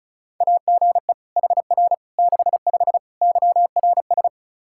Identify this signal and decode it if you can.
Morse code, heard as a keyed tone, about 36 wpm, 700 Hz. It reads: AGE HR 65 YRS
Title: Practice QSO - Exchange 3 - More Info - 35wpm